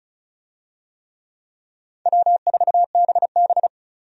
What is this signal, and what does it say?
Morse code, heard as a keyed tone, about 36 wpm, 700 Hz. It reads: W4BB
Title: USA Call Signs - 4 Characters - 35wpm